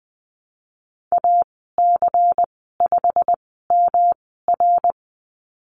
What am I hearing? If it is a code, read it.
Morse code, heard as a keyed tone, about 20 wpm, spaced as for 15 wpm, 700 Hz. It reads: AC5MR
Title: 147 Lesson 24g - Letter y - Call Signs 20wpm